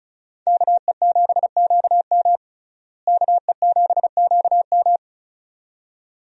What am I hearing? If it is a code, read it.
Morse code, heard as a keyed tone, about 34 wpm, 700 Hz. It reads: KE7QM KE7QM
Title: USA Call Signs - 5 Characters - Repeated 2x - 35wpm